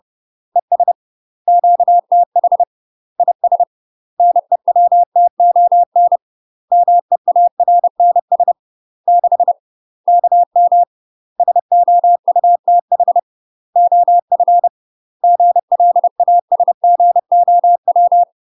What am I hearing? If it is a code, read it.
Morse code, heard as a keyed tone, about 30 wpm, 700 Hz. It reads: ES QTH IS NEWTON MEARNS 6 KM SOUTH OF GLASGOW
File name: UK MASTER QSOs TRAINING / 30WPM / UK MASTER QSOs 0044